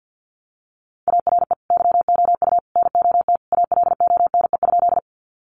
Morse code, heard as a keyed tone, about 50 wpm, 700 Hz. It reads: ARE YOU NOT ALONE?
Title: Sentences from Top 1000 Words 50wpm